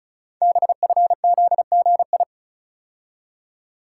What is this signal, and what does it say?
Morse code, heard as a keyed tone, about 36 wpm, 700 Hz. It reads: BFZGI
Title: Code Groups - 5 Letters - 35wpm